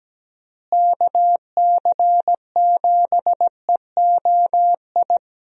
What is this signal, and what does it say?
Morse code, heard as a keyed tone, about 17 wpm, 700 Hz. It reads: KC7EOI
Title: USA Call Signs - 6 Characters - Repeated 2x - 17wpm